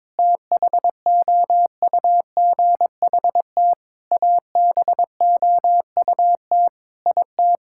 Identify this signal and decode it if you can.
Morse code, heard as a keyed tone, about 22 wpm, 700 Hz. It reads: THOUGHT ABOUT IT
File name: practice / 22wpm / Sentences with 8-Word Phrases - 22wpm